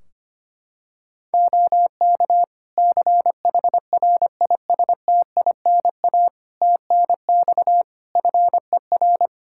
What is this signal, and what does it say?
Morse code, heard as a keyed tone, about 25 wpm, 700 Hz. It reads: OK CHRISTINA TNX FER
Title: Practice QSO - Exchange 4 - Ending - 25wpm